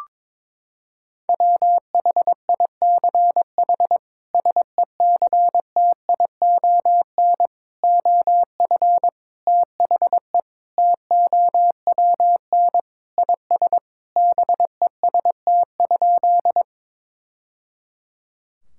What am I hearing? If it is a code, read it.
Morse code, heard as a keyed tone, about 22 wpm, 700 Hz. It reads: WHICH SECTION OF THE TOWN IS BEST?
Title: Sentences from Top 1000 Words - Review Part 5 - 22wpm